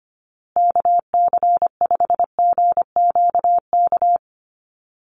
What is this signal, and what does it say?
Morse code, heard as a keyed tone, about 25 wpm, 700 Hz. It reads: KC5GQK
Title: USA Call Signs Repeated 2x 25wpm